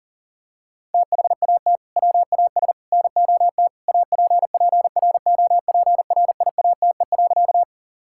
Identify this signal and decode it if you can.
Morse code, heard as a keyed tone, about 40 wpm, 700 Hz. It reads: THAT WAS NOT APPROPRIATE.
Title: Sentences from Top 1000 Words - Review Part 5 - 40wpm